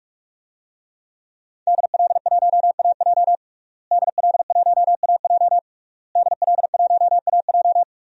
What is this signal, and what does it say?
Morse code, heard as a keyed tone, about 45 wpm, 700 Hz. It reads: DL1AJ DL1AJ DL1AJ
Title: DXCC-100 Call Signs - 5 Characters - Repeated 3x - 45wpm